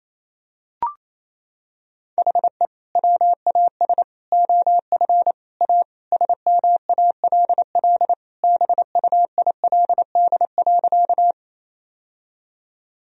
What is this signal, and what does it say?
Morse code, heard as a keyed tone, about 28 wpm, 700 Hz. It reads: HE WAS OF A SMALL BUILD.
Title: Sentences from Top 1000 Words - Review Part 1 - 28wpm